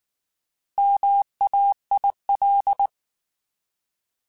Morse code, heard as a keyed tone, about 19 wpm, 800 Hz. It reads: MAIL
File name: Alphabetical / Alphabetical mp3s / Ms @18